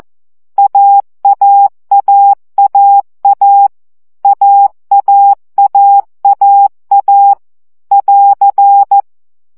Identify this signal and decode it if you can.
Morse code, heard as a keyed tone, about 14 wpm, 800 Hz. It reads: AAAAA AAAAA <AR>
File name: CW gyakorlas a 12 baud